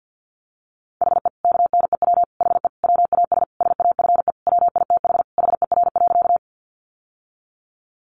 Morse code, heard as a keyed tone, about 50 wpm, 700 Hz. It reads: HE KNEW HE WAS SAFE WITH HER.
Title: Sentences from Top 900 Words 50wpm